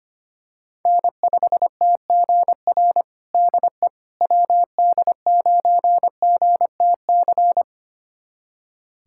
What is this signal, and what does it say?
Morse code, heard as a keyed tone, about 25 wpm, 700 Hz. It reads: N5TGR DE WD9GTC
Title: USA Callsign DE Callsign 25wpm